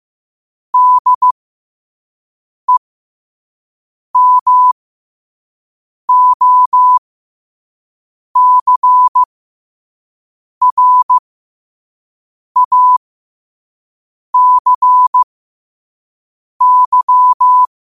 Morse code, heard as a keyed tone, about 15 wpm, spaced as for 5 wpm, 1000 Hz. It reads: DEMOCRACY
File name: qotdfiles / 5wpm